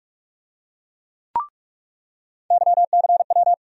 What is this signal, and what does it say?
Morse code, heard as a keyed tone, about 46 wpm, 700 Hz. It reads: YCW
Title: USA Call Sign Postfixes - 3 Letters 45wpm